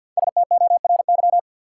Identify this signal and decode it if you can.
Morse code, heard as a keyed tone, about 52 wpm, 700 Hz. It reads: STORY
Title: Sets of 3 Words - New Word Drill - 50wpm